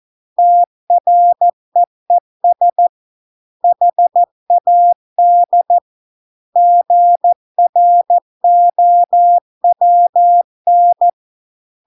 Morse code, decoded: TREES HAD GROWN

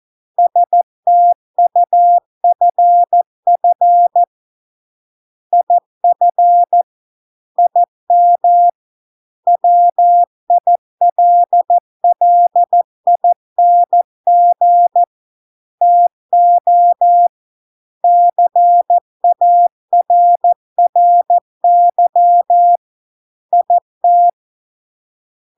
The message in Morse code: STUFF IF IM WILLING TO CARRY IT

